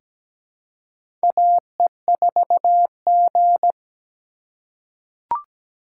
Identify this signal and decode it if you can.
Morse code, heard as a keyed tone, about 17 wpm, 700 Hz. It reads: AE4G
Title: USA Call Signs - 4x Character Spacing - 17wpm